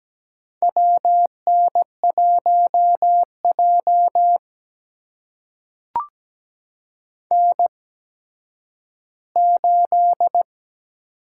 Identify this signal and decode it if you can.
Morse code, heard as a keyed tone, about 17 wpm, spaced as for 10 wpm, 700 Hz. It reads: WN1J N8
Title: USA Call Signs - 8x Character Spacing - 17wpm